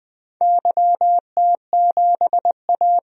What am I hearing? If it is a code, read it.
Morse code, heard as a keyed tone, about 21 wpm, 690 Hz. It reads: YT7A
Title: DXCC-100 Call Signs - 4 Characters - Repeated 3x - 20wpm